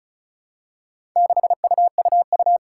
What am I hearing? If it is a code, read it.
Morse code, heard as a keyed tone, about 36 wpm, 690 Hz. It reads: 6UUU
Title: Four Challenging Letter-Number Combinations 35wpm